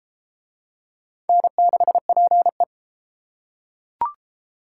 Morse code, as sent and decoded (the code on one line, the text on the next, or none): -. -.... .--. .
N6PE